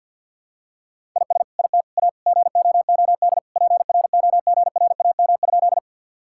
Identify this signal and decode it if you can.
Morse code, heard as a keyed tone, about 50 wpm, 700 Hz. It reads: IS IT A GOOD PROGRAM?
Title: Sentences from Top 1000 Words - Review Part 5 - 50wpm